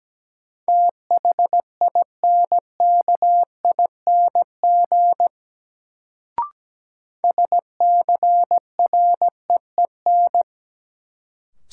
Morse code, heard as a keyed tone, about 17 wpm, 700 Hz. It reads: THINKING SCREEN
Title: Sentences from Top 1000 Words - Review Part 6 - New Word Drill - 17wpm